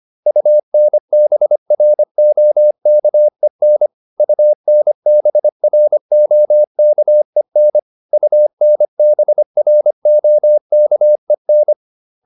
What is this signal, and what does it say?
Morse code, heard as a keyed tone, about 25 wpm, 600 Hz. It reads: UNBROKEN UNBROKEN UNBROKEN